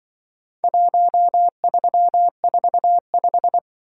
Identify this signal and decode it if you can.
Morse code, heard as a keyed tone, about 25 wpm, 700 Hz. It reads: 1345